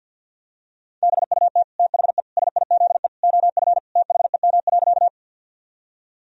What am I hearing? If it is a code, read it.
Morse code, heard as a keyed tone, about 52 wpm, 700 Hz. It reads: BUT THE SIZE OF THEM.
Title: Sentences from Top 1000 Words - Review Part 2 - 50wpm